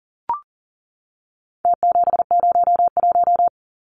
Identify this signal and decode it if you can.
Morse code, heard as a keyed tone, about 40 wpm, 700 Hz. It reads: T701
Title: Numbers - Contest Style - 4-Digit 40wpm